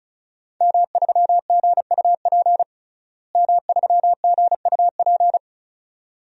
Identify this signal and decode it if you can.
Morse code, heard as a keyed tone, about 36 wpm, 700 Hz. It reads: M3GUP M3GUP
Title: DXCC-100 Call Signs - 5 Characters - Repeated 2x - 35wpm